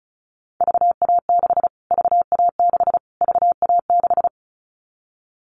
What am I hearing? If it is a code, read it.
Morse code, heard as a keyed tone, about 34 wpm, 700 Hz. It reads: VA6 VA6 VA6
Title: Callsign Prefixes for Top 25 DXCC - Repeated 3x - 35wpm